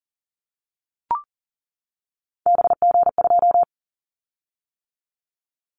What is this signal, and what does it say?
Morse code, heard as a keyed tone, about 40 wpm, 700 Hz. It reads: BG2
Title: Callsign Prefixes for Top 100 DXCC - 40wpm